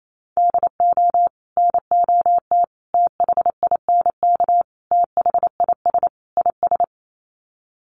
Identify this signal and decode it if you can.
Morse code, heard as a keyed tone, about 28 wpm, 700 Hz. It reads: DO NOT THINK THIS IS
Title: Sets of 5 Words Encore2 28wpm